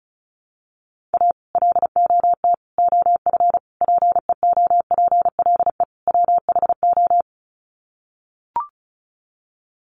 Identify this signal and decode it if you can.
Morse code, heard as a keyed tone, about 34 wpm, 700 Hz. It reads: A LOT OF PEOPLE WHO
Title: Sets of 5 Words 35wpm